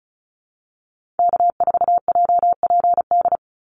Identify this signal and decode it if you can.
Morse code, heard as a keyed tone, about 34 wpm, 700 Hz. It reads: K4JPD
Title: Call Signs Repeated 2x - 35wpm